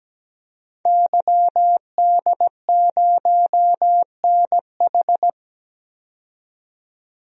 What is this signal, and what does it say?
Morse code, heard as a keyed tone, about 17 wpm, 700 Hz. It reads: YD0NH
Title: DXCC-100 Call Signs - 5 Characters - 17wpm